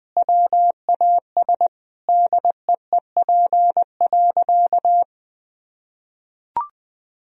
Morse code, heard as a keyed tone, about 20 wpm, 700 Hz. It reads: WAS DEEP.
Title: Sentences from Top 1000 Words - Review Part 3 - 20wpm